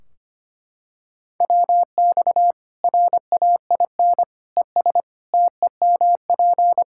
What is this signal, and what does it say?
Morse code, heard as a keyed tone, about 25 wpm, 700 Hz. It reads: WX RAIN ES TEMP
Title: Practice QSO - Exchange 2 - Info - 25wpm